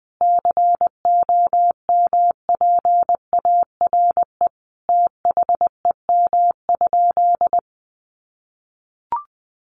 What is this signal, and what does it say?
Morse code, heard as a keyed tone, about 20 wpm, 700 Hz. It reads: COMPARE THEM?
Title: Sentences from Top 1300 Words 20wpm